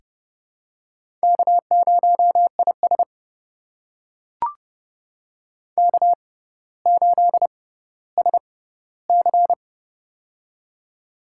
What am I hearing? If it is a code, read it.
Morse code, heard as a keyed tone, about 30 wpm, spaced as for 15 wpm, 700 Hz. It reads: K0IS K8SC
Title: USA Call Signs - 6x Character Spacing - 30wpm